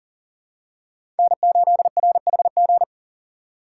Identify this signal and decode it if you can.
Morse code, heard as a keyed tone, about 40 wpm, 700 Hz. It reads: N8RHG
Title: USA Call Signs - 5 Characters - 4x Character Spacing - 40wpm